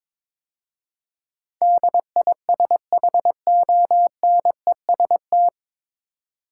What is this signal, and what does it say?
Morse code, heard as a keyed tone, about 22 wpm, 700 Hz. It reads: DISHONEST
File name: 9 Letter Words - First 3 Letters Given - 22wpm